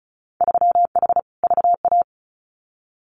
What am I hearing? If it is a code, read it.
Morse code, heard as a keyed tone, about 34 wpm, 700 Hz. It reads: 3H VA
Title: Winter Field Day Exchanges Encore 35wpm